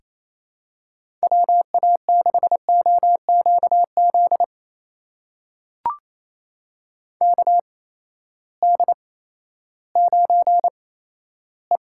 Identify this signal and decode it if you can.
Morse code, heard as a keyed tone, about 28 wpm, spaced as for 14 wpm, 700 Hz. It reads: WA6OQZ KD9E
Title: USA Call Signs - 6 Characters - 8x Character Spacing - 28wpm